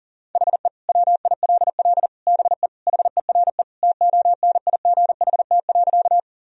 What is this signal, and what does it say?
Morse code, heard as a keyed tone, about 40 wpm, 700 Hz. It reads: HE WILL BE HERE TONIGHT.
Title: Sentences from Top 1000 Words - Review Part 3 - 40wpm